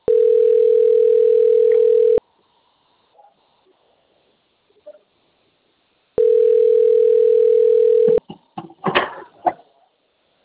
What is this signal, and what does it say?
Ringback tone (North America), 440 + 480 Hz.